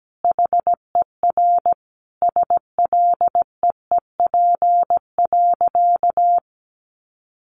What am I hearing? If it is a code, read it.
Morse code, heard as a keyed tone, about 17 wpm, 700 Hz. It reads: HER SLEEP.
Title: Sentences from Top 1200 Words 17wpm